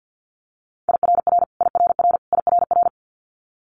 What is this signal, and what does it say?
Morse code, heard as a keyed tone, about 50 wpm, 700 Hz. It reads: IRR IRR IRR
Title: Multi-letter Phonemes Repeated 3x 50wpm